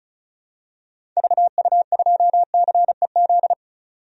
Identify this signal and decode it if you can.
Morse code, heard as a keyed tone, about 36 wpm, 700 Hz. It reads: VU2CEZ
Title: DXCC-100 Call Signs - 6 Characters - Repeated 2x - 35wpm